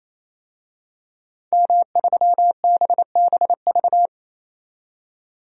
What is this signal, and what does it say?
Morse code, heard as a keyed tone, about 28 wpm, 700 Hz. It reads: M3BBV
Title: DXCC-100 Call Signs - 5 Characters - 28wpm